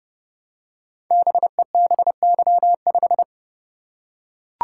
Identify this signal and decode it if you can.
Morse code, heard as a keyed tone, about 30 wpm, 700 Hz. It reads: BEBY5
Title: Code Groups - 5 Letter:Numbers - 30wpm